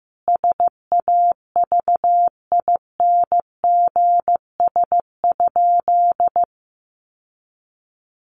Morse code, heard as a keyed tone, about 15 wpm, 700 Hz. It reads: SAVINGS?
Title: Sentences from Top 1200 Words 15wpm